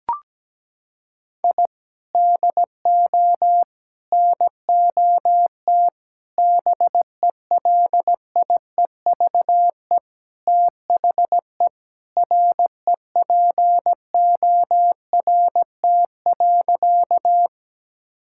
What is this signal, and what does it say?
Morse code, heard as a keyed tone, about 17 wpm, 700 Hz. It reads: I DO NOT BELIEVE THE REPORT.
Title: Sentences from Top 1000 Words - Review Part 6 - 17wpm